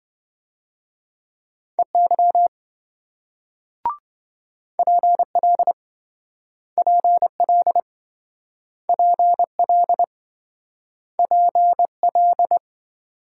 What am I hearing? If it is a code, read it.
Morse code, heard as a keyed tone, about 27 wpm, 700 Hz. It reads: EY PL PL PL PL